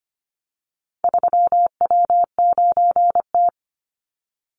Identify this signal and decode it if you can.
Morse code, heard as a keyed tone, about 25 wpm, 700 Hz. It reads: 3W9T
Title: Call Signs - 25wpm